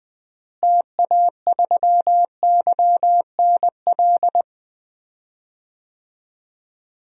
Morse code, heard as a keyed tone, about 20 wpm, 700 Hz. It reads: TA3YNL
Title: DXCC-100 Call Signs - 6 Characters - Repeated 2x - 20wpm